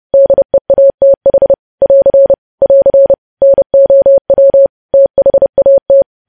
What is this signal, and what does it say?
Morse code, heard as a keyed tone, about 30 wpm, 550 Hz. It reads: DEATH <AR> <AR> NOW THAT
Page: news headlines in morse code at 30 wpm